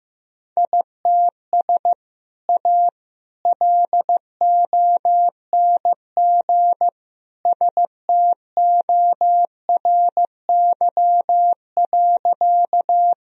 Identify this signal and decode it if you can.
Morse code, heard as a keyed tone, about 15 wpm, 700 Hz. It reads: ITS A LONG STORY.
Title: Sentences from Top 1000 Words - Review Part 1 - 15wpm